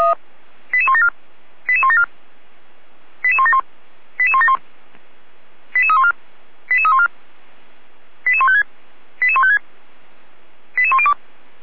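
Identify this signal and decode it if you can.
Five-tone selective call (ZVEI), heard as quick runs of five tones, tones of about 70 ms. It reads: 88174 88174 88171 88171 88314 88314 88156 88156 88102
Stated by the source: ZVEI five-tone selective call received on SDR, logged by feuerwehr-orlamuende.org (Feuerwehr Orlamünde fire brigade).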